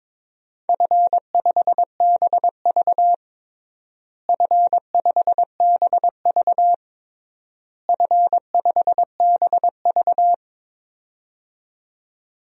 Morse code, heard as a keyed tone, about 22 wpm, 700 Hz. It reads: F5BV F5BV F5BV
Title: DXCC-100 Call Signs - 4 Characters - Repeated 3x - 22wpm